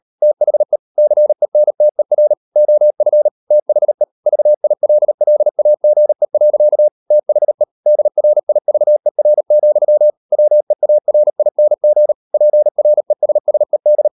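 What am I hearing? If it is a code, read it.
Morse code, heard as a keyed tone, about 38 wpm, 600 Hz. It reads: THE CENTER OF THE VILLAGE. THE DRIVER, WEARING PRESSED
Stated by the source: Morse code practice at 38 WPM, sending Ken Follett's novel Never in English, from CW med Gnister.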